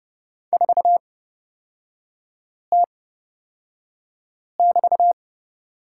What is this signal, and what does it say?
Morse code, heard as a keyed tone, about 31 wpm, spaced as for 5 wpm, 700 Hz. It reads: 4T<BT>
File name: Lesson 38e - Letter q - Three Characters 30wpm